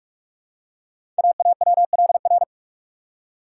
Morse code, heard as a keyed tone, about 44 wpm, 690 Hz. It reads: AAWLR
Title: Code Groups - 5 Letters - 45wpm